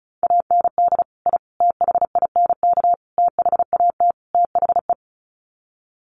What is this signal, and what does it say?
Morse code, heard as a keyed tone, about 34 wpm, 700 Hz. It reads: AND I THINK THAT THE
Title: Sets of 5 Words 35wpm